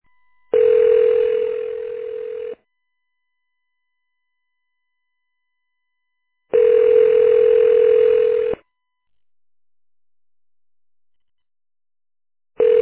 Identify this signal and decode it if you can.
Ringback tone (North America), 440 + 480 Hz.